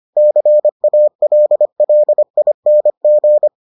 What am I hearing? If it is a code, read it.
Morse code, heard as a keyed tone, about 25 wpm, 600 Hz. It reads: CALLING